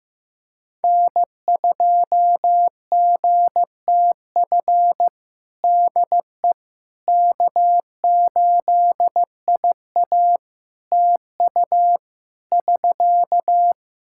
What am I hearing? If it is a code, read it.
Morse code, heard as a keyed tone, about 15 wpm, 700 Hz. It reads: N2GTF DE K8IA TU <SK>
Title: Practice QSO - Exchange 4 - Ending - 15wpm